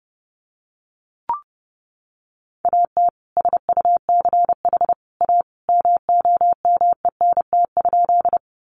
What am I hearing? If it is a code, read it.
Morse code, heard as a keyed tone, about 30 wpm, 700 Hz. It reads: AT SUCH A MOMENT?
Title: Sentences from Top 1000 Words 30wpm